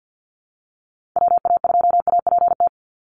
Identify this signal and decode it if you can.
Morse code, heard as a keyed tone, about 50 wpm, 700 Hz. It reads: WA2APT
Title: USA Call Signs 50wpm